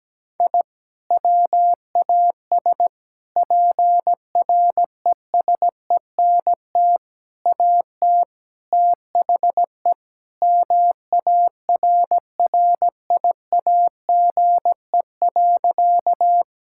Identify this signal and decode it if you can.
Morse code, heard as a keyed tone, about 17 wpm, 700 Hz. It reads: I WAS PRESENT AT THE MARRIAGE.
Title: Sentences from Top 1000 Words - Review Part 5 - 17wpm